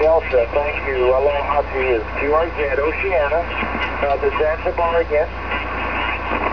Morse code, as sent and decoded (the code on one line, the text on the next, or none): none